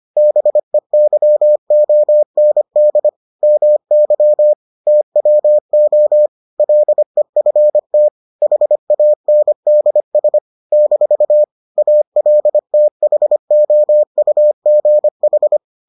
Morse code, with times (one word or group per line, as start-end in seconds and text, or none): 0.2-3.1 BEYOND
3.4-4.5 MY
4.9-6.3 TWO
6.6-8.1 LEFT
8.4-10.4 HANDS
10.7-11.4 -
11.8-15.6 ALTHOUGH